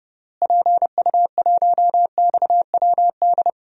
Morse code, heard as a keyed tone, about 29 wpm, 700 Hz. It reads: PU1XWD
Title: DXCC-100 Call Signs - 6 Characters - Repeated 2x - 30wpm